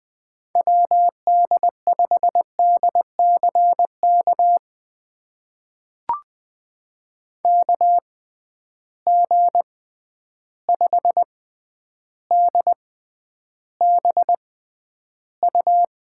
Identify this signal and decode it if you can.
Morse code, heard as a keyed tone, about 20 wpm, spaced as for 10 wpm, 700 Hz. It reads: WD5DCK KG5DBU
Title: USA Call Signs - 6 Characters - 6x Character Spacing - 20wpm